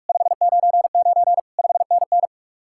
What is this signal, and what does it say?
Morse code, heard as a keyed tone, about 57 wpm, 700 Hz. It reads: 599 5NN